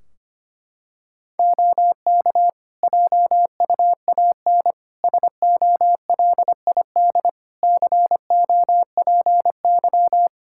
Morse code, heard as a keyed tone, about 25 wpm, 700 Hz. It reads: OK JUAN SOLID COPY
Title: K1USN SST QSO Practice 25wpm